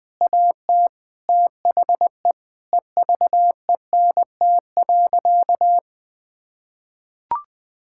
Morse code, heard as a keyed tone, about 20 wpm, 700 Hz. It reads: AT THE EVENT.
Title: Sentences with 7-Word Phrases - Spoken Beginning - 20wpm